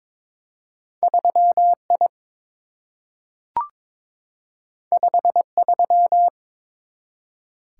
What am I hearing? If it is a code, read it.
Morse code, heard as a keyed tone, about 22 wpm, 700 Hz. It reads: 3I 53
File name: practice / 22wpm / Two Letter-Number Combinations - 22wpm